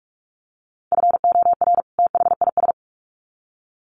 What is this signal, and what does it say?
Morse code, heard as a keyed tone, about 44 wpm, 700 Hz. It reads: FOR THIS
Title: Sets of 2 Words 45wpm